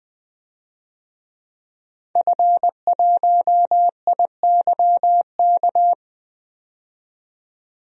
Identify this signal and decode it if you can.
Morse code, heard as a keyed tone, about 20 wpm, 700 Hz. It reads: F1IYK
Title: DXCC-100 Call Signs - 5 Characters - 20wpm